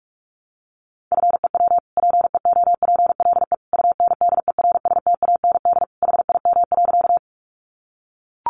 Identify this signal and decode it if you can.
Morse code, heard as a keyed tone, about 44 wpm, 700 Hz. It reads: FEW PEOPLE UNDERSTAND HIM.
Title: Sentences from Top 600 Words 45wpm